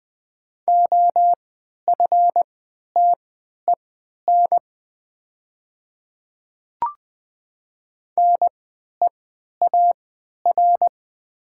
Morse code, heard as a keyed tone, about 20 wpm, spaced as for 11 wpm, 700 Hz. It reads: OFTEN NEAR
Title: 3x Character Spacing - Top 300 Words - 20wpm